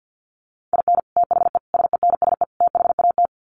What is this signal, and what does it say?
Morse code, heard as a keyed tone, about 50 wpm, 700 Hz. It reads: IN THE SENSE THAT
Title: Sets of 4 Words 50wpm